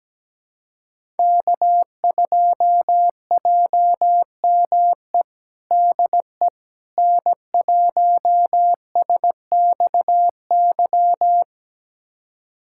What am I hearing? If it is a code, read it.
Morse code, heard as a keyed tone, about 17 wpm, 700 Hz. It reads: K2JME DE N1SXY
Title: USA Callsign DE Callsign - 5 Characters - 17wpm